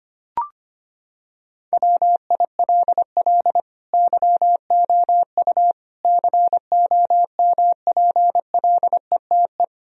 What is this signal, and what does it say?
Morse code, heard as a keyed tone, about 25 wpm, 700 Hz. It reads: WILL YOU COMPLETE
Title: Sentences from Top 1000 Words - Review Part 2 - 25wpm